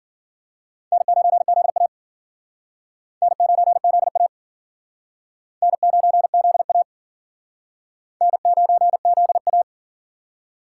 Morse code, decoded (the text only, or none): N9ZA N9ZA N9ZA N9ZA